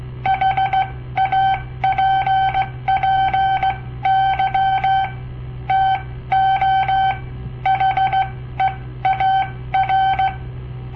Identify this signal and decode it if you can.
Morse code, heard as a keyed tone, about 17 wpm, spaced as for 14 wpm, 770 Hz. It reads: HAPPY TO HEAR
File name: papertapes / local / Code Practice-01